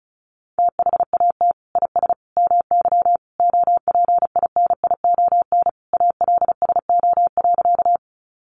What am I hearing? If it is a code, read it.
Morse code, heard as a keyed tone, about 34 wpm, 700 Hz. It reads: THAT IS MY OPINION ALSO.
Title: Sentences from Top 1200 Words 35wpm